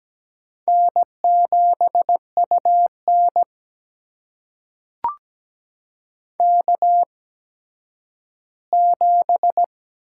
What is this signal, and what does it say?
Morse code, heard as a keyed tone, about 17 wpm, spaced as for 10 wpm, 700 Hz. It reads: N7UN K7
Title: USA Call Signs - 4 Characters - 8x Character Spacing - 17wpm